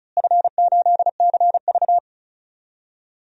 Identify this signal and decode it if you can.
Morse code, heard as a keyed tone, about 36 wpm, 700 Hz. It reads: F8CV
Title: DXCC-100 Call Signs - 4 Characters - Repeated 3x - 35wpm